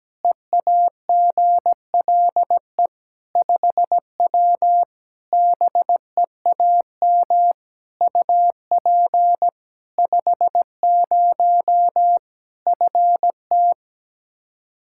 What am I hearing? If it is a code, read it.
Morse code, heard as a keyed tone, about 17 wpm, 700 Hz. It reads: EAGLE 5W BEAM UP 50 FT
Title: Practice QSO - Exchange 2 - Info - 17wpm